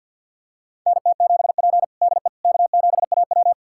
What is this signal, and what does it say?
Morse code, heard as a keyed tone, about 51 wpm, 700 Hz. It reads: NT7P DE K7AW
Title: USA Callsign DE Callsign - 4 Characters - 50wpm